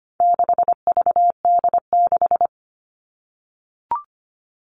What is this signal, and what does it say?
Morse code, heard as a keyed tone, about 25 wpm, 700 Hz. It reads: 6VD6
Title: All Cut Numbers - 4-Digit 25wpm